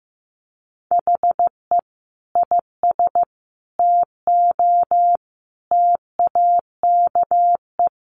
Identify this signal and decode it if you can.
Morse code, heard as a keyed tone, about 15 wpm, 700 Hz. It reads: HE IS TO TAKE
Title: Sentences from Top 1400 Words 15wpm